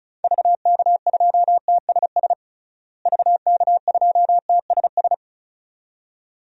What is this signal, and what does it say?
Morse code, heard as a keyed tone, about 36 wpm, 700 Hz. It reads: VK2TSS VK2TSS
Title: DXCC-100 Call Signs - 6 Characters - Repeated 2x - 35wpm